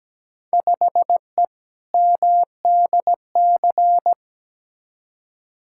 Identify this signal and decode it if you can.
Morse code, heard as a keyed tone, about 17 wpm, 700 Hz. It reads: 5E MDC
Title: Field Day Exchanges - 17wpm